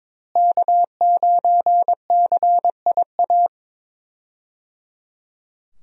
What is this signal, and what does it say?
Morse code, heard as a keyed tone, about 22 wpm, 700 Hz. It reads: K9CIA
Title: USA Call Signs - 5 Characters - Repeated 3x - 22wpm